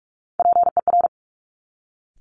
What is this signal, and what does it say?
Morse code, heard as a keyed tone, about 46 wpm, 700 Hz. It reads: PER
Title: Top Three Letter Combinations in Words 45wpm